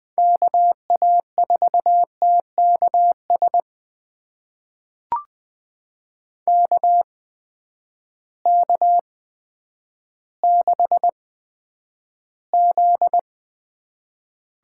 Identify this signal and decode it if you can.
Morse code, heard as a keyed tone, about 20 wpm, spaced as for 10 wpm, 700 Hz. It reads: KA4TKS KK6Z
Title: USA Call Signs - 8x Character Spacing - 20wpm